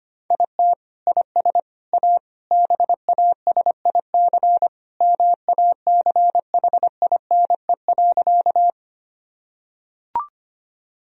Morse code, heard as a keyed tone, about 25 wpm, 700 Hz. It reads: IT IS A BASIC MACHINE.
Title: Sentences from Top 1000 Words - Review Part 3 - 25wpm